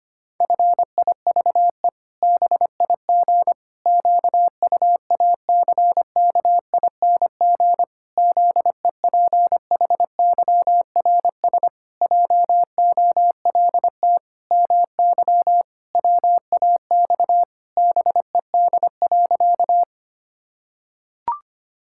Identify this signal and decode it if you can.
Morse code, heard as a keyed tone, about 25 wpm, 700 Hz. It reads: FIVE BIG QUACKING ZEPHYRS JOLT MY WAX BED.
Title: Pangram Challenge v2 - 25wpm